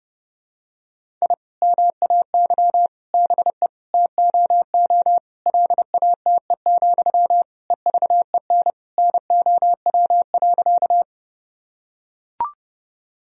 Morse code, heard as a keyed tone, about 30 wpm, 700 Hz. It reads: I MAY BE TOO LATE, EVEN NOW.
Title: Sentences from Top 1000 Words - Review Part 1 - 30wpm